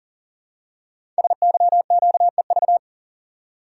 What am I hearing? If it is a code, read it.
Morse code, heard as a keyed tone, about 39 wpm, 700 Hz. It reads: SYQEV
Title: Code Groups - 5 Letters - 40wpm